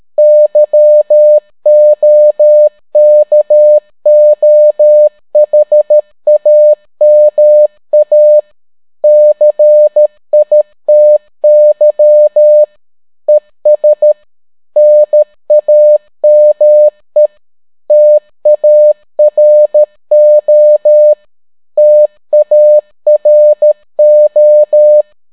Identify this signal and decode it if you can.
Morse code, heard as a keyed tone, about 13 wpm, 600 Hz. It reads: YOKOHAMA CITY ES NAME TARO TARO